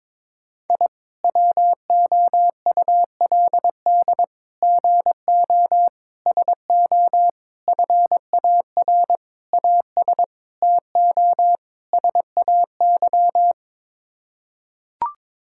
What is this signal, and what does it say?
Morse code, heard as a keyed tone, about 22 wpm, 700 Hz. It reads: I WOULD GO SO FAR AS TO SAY
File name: practice / 22wpm / Sets of 8 Words - 22wpm